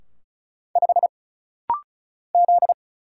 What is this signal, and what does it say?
Morse code, heard as a keyed tone, about 33 wpm, 700 Hz. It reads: 5 Z
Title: Single Letter-Number Alt - Rapid-Fire 35wpm